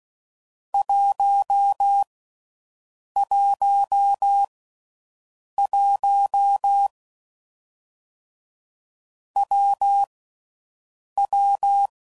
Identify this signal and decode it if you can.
Morse code, heard as a keyed tone, about 16 wpm, 800 Hz. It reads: 1 1 1 W W
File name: Combinations / Difficult Combos / WPJ WPJ1